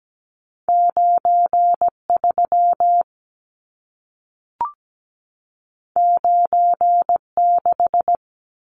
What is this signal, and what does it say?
Morse code, heard as a keyed tone, about 17 wpm, 700 Hz. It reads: 93 96
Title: Numbers - 2-Digit 17wpm